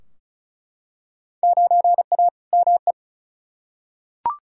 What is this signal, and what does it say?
Morse code, heard as a keyed tone, about 36 wpm, 700 Hz. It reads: 9A ME
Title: Field Day Exchanges Encore - 35wpm